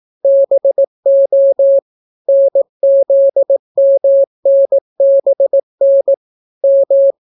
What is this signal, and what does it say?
Morse code, heard as a keyed tone, about 18 wpm, 550 Hz. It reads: BO NZMNBN M